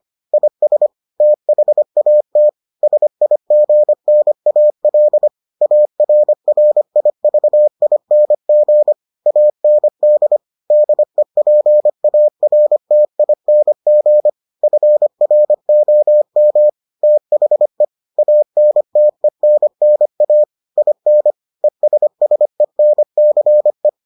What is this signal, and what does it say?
Morse code, heard as a keyed tone, about 25 wpm, 600 Hz. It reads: IS THAT SIGNAL ARRIVING AND DEPARTING FROM THE ANTENNA IN ESSENCE